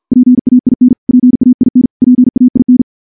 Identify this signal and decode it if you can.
Morse code, heard as a keyed tone, about 50 wpm, 250 Hz. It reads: PAIN PAIN PAIN
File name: morse / morse-training-50WPM